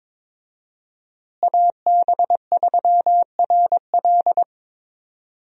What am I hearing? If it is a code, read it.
Morse code, heard as a keyed tone, about 22 wpm, 700 Hz. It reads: AB3RL
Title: USA Call Signs - 5 Characters - Repeated 3x - 22wpm